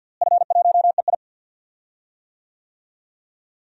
Morse code, heard as a keyed tone, about 51 wpm, 700 Hz. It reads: F1EI
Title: DXCC-100 Call Signs - 4 Characters - Repeated 2x - 50wpm